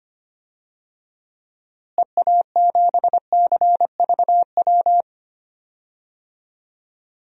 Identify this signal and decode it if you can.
Morse code, heard as a keyed tone, about 25 wpm, 700 Hz. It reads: EA7CVW
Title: DXCC-100 Call Signs - 6 Characters - Repeated 3x - 25wpm